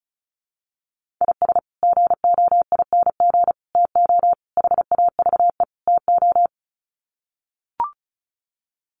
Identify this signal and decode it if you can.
Morse code, heard as a keyed tone, about 34 wpm, 700 Hz. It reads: IS GOING TO HAVE TO
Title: Sets of 5 Words 35wpm